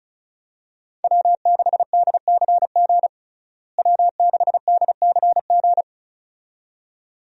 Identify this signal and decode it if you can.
Morse code, heard as a keyed tone, about 36 wpm, 700 Hz. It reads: W6DCG W6DCG
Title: USA Call Signs - 5 Characters - Repeated 2x - 35wpm